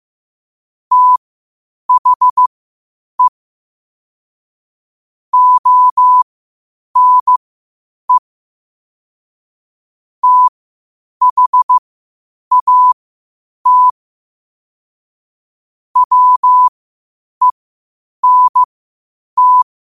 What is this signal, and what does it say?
Morse code, heard as a keyed tone, about 15 wpm, spaced as for 8 wpm, 1000 Hz. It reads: THE ONE THAT WENT
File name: qotdfiles / 8wpm